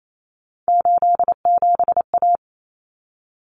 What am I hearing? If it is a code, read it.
Morse code, heard as a keyed tone, about 28 wpm, 700 Hz. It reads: 87A
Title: Common Cut Numbers - 3-Digit 28wpm